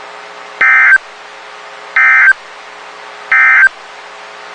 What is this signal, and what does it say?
Emergency Alert System (SAME) tones, heard as harsh buzzing bursts. It is heard from 0.5 to 3.8 s.
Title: WXK91 NOAA Weather Radio